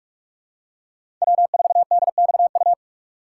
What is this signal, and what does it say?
Morse code, heard as a keyed tone, about 46 wpm, 700 Hz. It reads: W4DXU
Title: USA Call Signs - 5 Characters - 2x Character Spacing - 45wpm